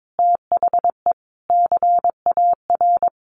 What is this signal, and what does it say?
Morse code, heard as a keyed tone, about 22 wpm, 700 Hz. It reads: THE CAR
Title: Sets of 5 Words Encore 22wpm